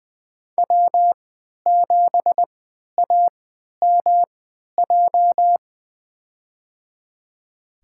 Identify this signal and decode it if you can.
Morse code, heard as a keyed tone, about 20 wpm, spaced as for 11 wpm, 700 Hz. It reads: W7AMJ
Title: USA Call Signs - 5 Characters - 3x Character Spacing - 20wpm